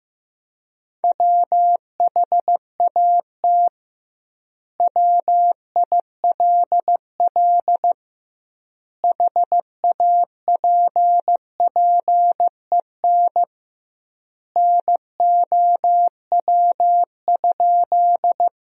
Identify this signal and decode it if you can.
Morse code, heard as a keyed tone, about 16 wpm, 700 Hz. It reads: WHAT WILL HAPPEN NOW?
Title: 2x Word-Spacing - Sentences from Top 400 Words - 15wpm